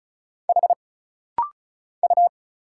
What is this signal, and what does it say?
Morse code, heard as a keyed tone, about 34 wpm, 700 Hz. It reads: H U
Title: Single Letter-Number - Rapid-Fire 35wpm